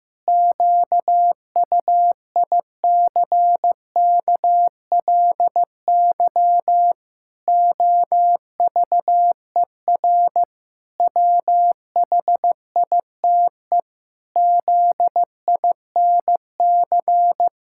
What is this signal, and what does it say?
Morse code, heard as a keyed tone, about 14 wpm, 700 Hz. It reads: QUICKLY OVER WHITE ZINC
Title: Pangram Challenge v2 - 15wpm